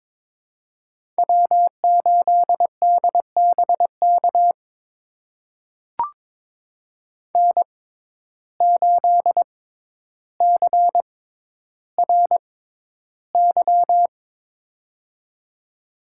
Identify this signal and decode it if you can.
Morse code, heard as a keyed tone, about 22 wpm, spaced as for 11 wpm, 700 Hz. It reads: W8DBK N8CRY
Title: USA Call Signs - 6x Character Spacing - 22wpm